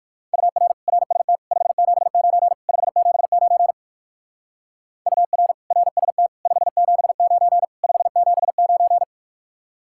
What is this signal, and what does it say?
Morse code, heard as a keyed tone, about 54 wpm, 700 Hz. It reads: UR RST 579 579 UR RST 579 579
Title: Speed-Racing Common QSO Phrases - 45wpm